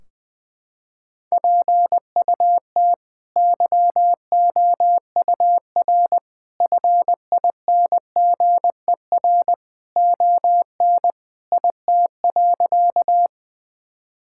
Morse code, heard as a keyed tone, about 20 wpm, 700 Hz. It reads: PUT YOUR FINGER ON IT.
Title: Sentences from Top 1600 Words 20wpm